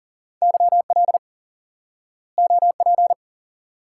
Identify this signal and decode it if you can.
Morse code, heard as a keyed tone, about 38 wpm, 700 Hz. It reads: YL OP
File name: IQE Course 3 - Lesson 89 - YAGI - Two Elements - 40wpm